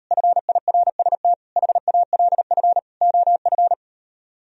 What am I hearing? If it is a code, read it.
Morse code, heard as a keyed tone, about 38 wpm, 700 Hz. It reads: FIRST HALF OF